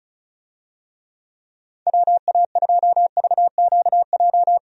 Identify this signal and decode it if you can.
Morse code, heard as a keyed tone, about 36 wpm, 700 Hz. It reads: WA2VQJ
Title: USA Call Signs - 6 Characters - 35wpm